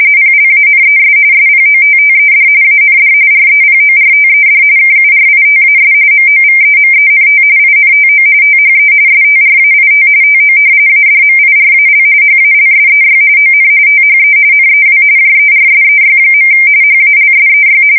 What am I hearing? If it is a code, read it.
RTTY, about 45.45 baud. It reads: PLANES SPOTTED AND REPORTED KURITAS MOVEMENT INTO THE STRAIT TO ADMIRAL HALSEY ON THE NEW JERSEY. HALSEY
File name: fsk170 / battleshipnj